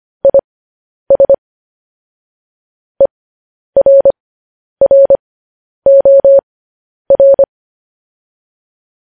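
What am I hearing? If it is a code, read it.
Morse code, heard as a keyed tone, about 25 wpm, spaced as for 10 wpm, 550 Hz. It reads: IS ERROR
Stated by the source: Morse code sending of news headlines, speed 10.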